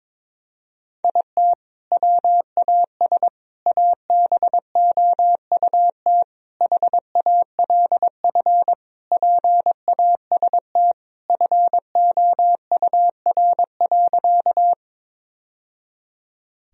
Morse code, heard as a keyed tone, about 22 wpm, 700 Hz. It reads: IT WAS ABOUT HALF PAST FOUR.
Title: Sentences from Top 1000 Words - Review Part 3 - 22wpm